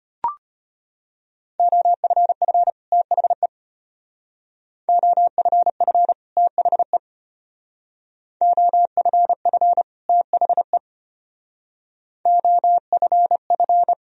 Morse code, heard as a keyed tone, about 33 wpm, 700 Hz. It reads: OFF THE OFF THE OFF THE OFF